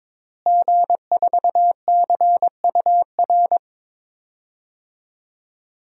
Morse code, heard as a keyed tone, about 22 wpm, 700 Hz. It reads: G4CUR
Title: DXCC-100 Call Signs - 5 Characters - Repeated 2x - 22wpm